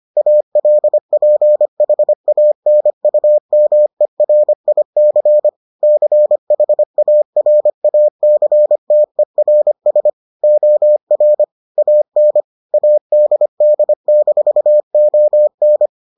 Morse code, heard as a keyed tone, about 25 wpm, 600 Hz. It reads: ALPHANUMERIC CHARACTERS OR AN ADD-ON